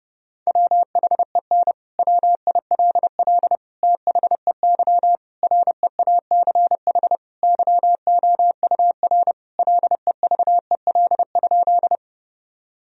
Morse code, heard as a keyed tone, about 30 wpm, 700 Hz. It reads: WHEN WILL THEY REACH YOUR LEVEL?
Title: Sentences from Top 1000 Words - Review Part 4 - 30wpm